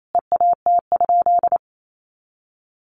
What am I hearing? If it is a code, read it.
Morse code, heard as a keyed tone, about 28 wpm, 700 Hz. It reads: EAT?
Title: Sentences from Top 500 Words 28wpm